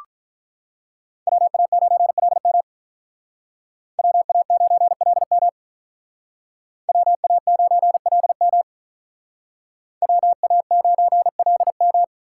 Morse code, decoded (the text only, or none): WA9LM WA9LM WA9LM WA9LM